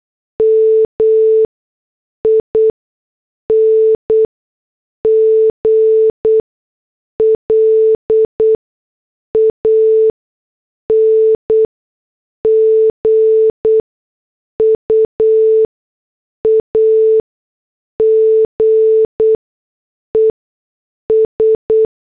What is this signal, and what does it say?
Morse code, heard as a keyed tone, about 8 wpm, spaced as for 6 wpm, 440 Hz. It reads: MING LANGUAGES
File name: morse / headlines